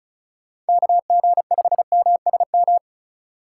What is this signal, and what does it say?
Morse code, heard as a keyed tone, about 34 wpm, 700 Hz. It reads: KG5MSM
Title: USA Call Signs - 6 Characters - 4x Character Spacing - 35wpm